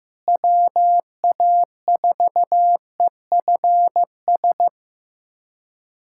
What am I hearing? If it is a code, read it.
Morse code, heard as a keyed tone, about 15 wpm, 700 Hz. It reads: WA4EFS
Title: USA Callsign DE Callsign - 6 Characters - 15wpm